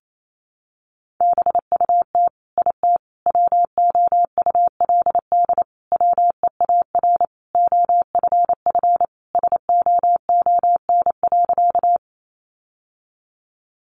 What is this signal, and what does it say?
Morse code, heard as a keyed tone, about 28 wpm, 700 Hz. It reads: BUT IT WOULD WEAR OFF SOON.
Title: Sentences from Top 1500 Words 28wpm